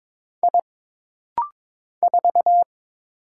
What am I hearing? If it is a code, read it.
Morse code, heard as a keyed tone, about 22 wpm, 700 Hz. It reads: I 4
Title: Single Letter-Number - Rapid-Fire 22wpm